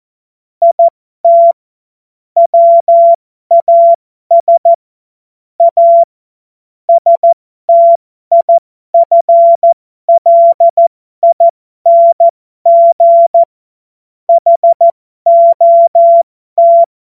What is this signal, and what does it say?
Morse code, heard as a keyed tone, about 14 wpm, spaced as for 12 wpm, 680 Hz. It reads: IT WAS A STIFLING HOT
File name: Morsekurs / Witze